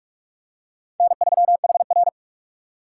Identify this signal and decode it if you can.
Morse code, heard as a keyed tone, about 42 wpm, 700 Hz. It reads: N3HR